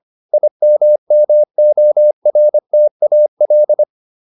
Morse code, heard as a keyed tone, about 25 wpm, 600 Hz. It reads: IMMORTAL